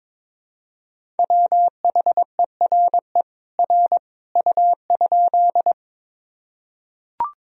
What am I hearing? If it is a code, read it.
Morse code, heard as a keyed tone, about 22 wpm, 700 Hz. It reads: WHERE R U?
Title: Sentences from Top 200 Words and Abbreviations 22wpm